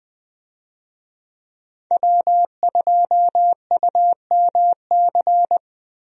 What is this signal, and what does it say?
Morse code, heard as a keyed tone, about 20 wpm, 700 Hz. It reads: W2UMC
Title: USA Call Signs - 5 Characters - Repeated 3x - 20wpm